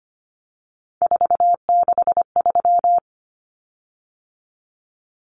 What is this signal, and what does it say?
Morse code, heard as a keyed tone, about 25 wpm, 700 Hz. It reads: 463
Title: Common Cut Numbers - 3-Digit 25wpm